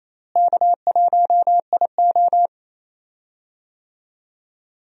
Morse code, heard as a keyed tone, about 27 wpm, 700 Hz. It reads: K1IO
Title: USA Call Signs - 4 Characters - Repeated 2x - 28wpm